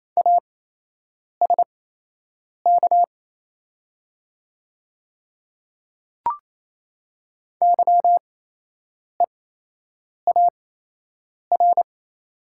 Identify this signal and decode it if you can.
Morse code, heard as a keyed tone, about 28 wpm, spaced as for 8 wpm, 700 Hz. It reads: ASK YEAR
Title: 8x Character Spacing - Top 300 Words - 28wpm